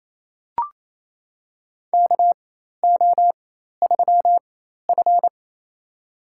Morse code, heard as a keyed tone, about 28 wpm, spaced as for 13 wpm, 700 Hz. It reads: KO3F
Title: USA Call Signs - 4 Characters - 4x Character Spacing - 28wpm